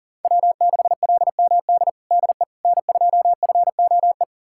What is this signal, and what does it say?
Morse code, heard as a keyed tone, about 40 wpm, 700 Hz. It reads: W6LMD DE N2FOE
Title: USA Callsign DE Callsign - 5 Characters - 40wpm